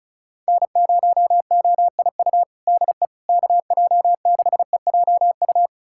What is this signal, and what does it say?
Morse code, heard as a keyed tone, about 34 wpm, 700 Hz. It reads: N0OIU DE KJ6EJU
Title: USA Callsign DE Callsign 35wpm